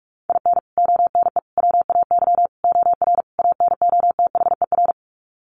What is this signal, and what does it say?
Morse code, heard as a keyed tone, about 44 wpm, 700 Hz. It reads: IN ONE WAY OR ANOTHER
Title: Sets of 5 Words Encore 45wpm